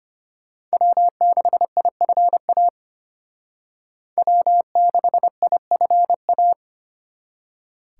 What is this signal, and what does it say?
Morse code, heard as a keyed tone, about 29 wpm, 700 Hz. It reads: W6IFA W6IFA
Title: Speed Racing - USA 1x3 Call Signs 25wpm